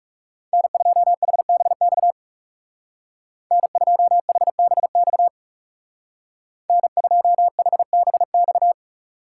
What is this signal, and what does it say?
Morse code, heard as a keyed tone, about 35 wpm, 700 Hz. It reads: N2HBX N2HBX N2HBX
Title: Speed Racing - USA 1x3 Call Signs 30wpm